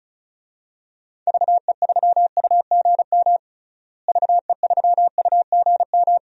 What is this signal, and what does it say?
Morse code, heard as a keyed tone, about 34 wpm, 700 Hz. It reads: VE3UGM VE3UGM
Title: DXCC-100 Call Signs - 6 Characters - Repeated 2x - 35wpm